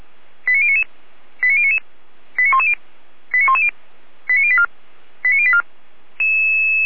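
Five-tone selective call (ZVEI), heard as quick runs of five tones, tones of about 70 ms. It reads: 89099 89099 89110 89110 89084 89084
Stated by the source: ZVEI five-tone selective call received with an SDR.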